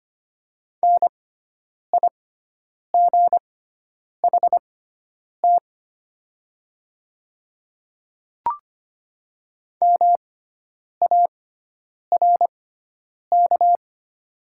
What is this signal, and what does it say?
Morse code, heard as a keyed tone, about 25 wpm, spaced as for 9 wpm, 700 Hz. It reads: NIGHT MARK
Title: 6x Character Spacing - Top 300 Words - 25wpm